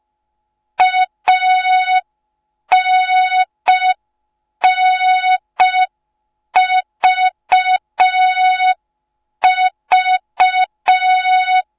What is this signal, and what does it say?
Morse code, heard as a keyed tone, about 5 wpm, 3700 Hz. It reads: ANNVV